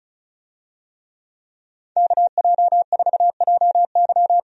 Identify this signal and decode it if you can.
Morse code, heard as a keyed tone, about 34 wpm, 700 Hz. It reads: KJ4JY
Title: USA Call Signs - 5 Characters - Repeated 2x - 35wpm